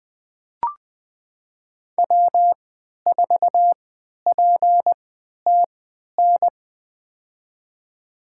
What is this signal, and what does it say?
Morse code, heard as a keyed tone, about 21 wpm, spaced as for 12 wpm, 700 Hz. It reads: W4PTN
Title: USA Call Signs - 5 Characters - 3x Character Spacing - 20wpm